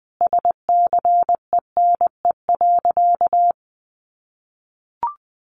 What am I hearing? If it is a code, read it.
Morse code, heard as a keyed tone, about 20 wpm, 700 Hz. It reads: SCENE.
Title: Sentences from Top 1100 Words 20wpm